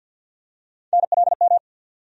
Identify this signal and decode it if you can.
Morse code, heard as a keyed tone, about 52 wpm, 700 Hz. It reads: NLM
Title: USA Call Sign Postfixes - 3 Letters 50wpm